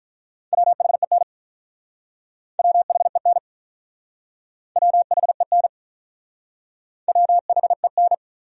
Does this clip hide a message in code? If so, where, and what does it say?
Yes: Morse code from 0.5 to 8.1 s, heard as a keyed tone, about 47 wpm, 700 Hz. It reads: WHEN WHEN WHEN WHEN